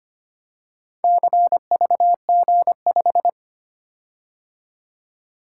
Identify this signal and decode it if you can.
Morse code, heard as a keyed tone, about 25 wpm, 700 Hz. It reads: CVG5
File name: practice / 25wpm / Four Letter-Number Combinations - 25wpm